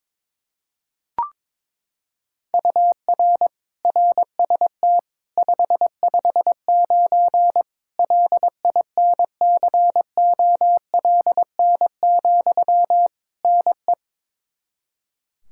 Morse code, heard as a keyed tone, about 21 wpm, 700 Hz. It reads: UR RST 559 LINCOLN, NE
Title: Practice QSO - Exchange 1 - Beginning - 22wpm